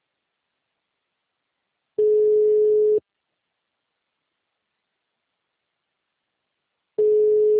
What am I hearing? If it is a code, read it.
Ringback tone (Europe, 425 Hz).
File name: out- -2004- - - .164